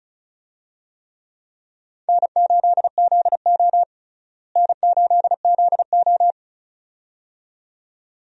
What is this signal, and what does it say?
Morse code, heard as a keyed tone, about 36 wpm, 700 Hz. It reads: N8ZO N8ZO
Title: USA Call Signs - 4 Characters - Repeated 2x - 35wpm